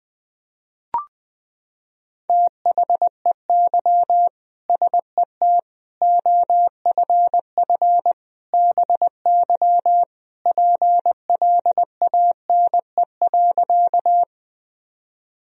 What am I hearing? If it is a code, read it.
Morse code, heard as a keyed tone, about 20 wpm, 700 Hz. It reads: THEY SET OFF BY PLANE.
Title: Sentences from Top 1000 Words - Review Part 3 - 20wpm